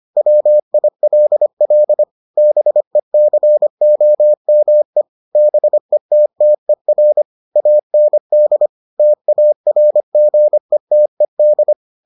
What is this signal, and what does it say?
Morse code, heard as a keyed tone, about 25 wpm, 600 Hz. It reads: WILL BECOME BETTER AND TARGETED